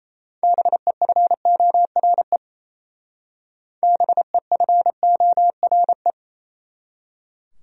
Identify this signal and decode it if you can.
Morse code, heard as a keyed tone, about 37 wpm, 700 Hz. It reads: BEFORE BEFORE